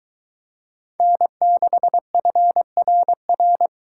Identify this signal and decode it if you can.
Morse code, heard as a keyed tone, about 23 wpm, 700 Hz. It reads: N6FRR